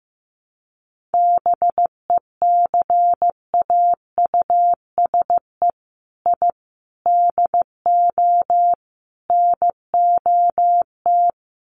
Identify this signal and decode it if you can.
Morse code, heard as a keyed tone, about 15 wpm, 700 Hz. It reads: BECAUSE I DO NOT
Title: Sets of 4 Words 15wpm